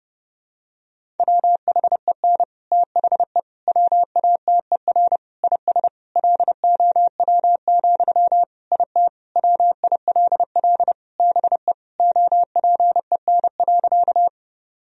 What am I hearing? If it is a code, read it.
Morse code, heard as a keyed tone, about 30 wpm, 700 Hz. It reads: WHEN THE WATER IS LOW, IT WILL BE OPEN.
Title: Sentences from Top 1000 Words - Review Part 1 - 30wpm